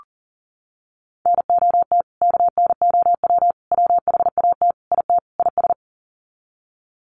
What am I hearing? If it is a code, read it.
Morse code, heard as a keyed tone, about 40 wpm, 700 Hz. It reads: NOT KNOW WHAT IT IS
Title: Sets of 5 Words Encore 40wpm